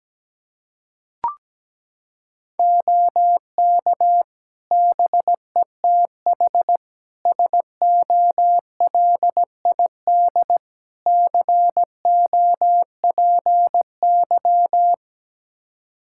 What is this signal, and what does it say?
Morse code, heard as a keyed tone, about 17 wpm, 700 Hz. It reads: OK BETH SOLID COPY
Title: Practice QSO - Exchange 3 - More Info - 17wpm